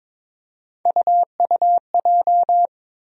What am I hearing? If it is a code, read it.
Morse code, heard as a keyed tone, about 22 wpm, 700 Hz. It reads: UUJ
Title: USA Call Sign Postfixes - 3 Letters 22wpm